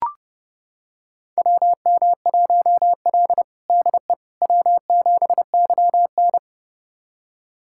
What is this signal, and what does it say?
Morse code, heard as a keyed tone, about 30 wpm, 700 Hz. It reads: WM1L DE W7YN
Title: USA Callsign DE Callsign - 4 Characters - 30wpm